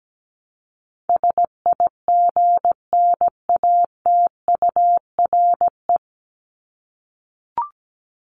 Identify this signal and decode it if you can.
Morse code, heard as a keyed tone, about 17 wpm, 700 Hz. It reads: SIGNATURE
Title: Top 1751-2000 Words 17wpm